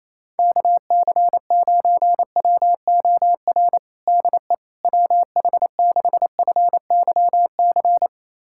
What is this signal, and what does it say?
Morse code, heard as a keyed tone, about 28 wpm, 700 Hz. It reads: KC9WOR DE WH6FYC
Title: USA Callsign DE Callsign 28wpm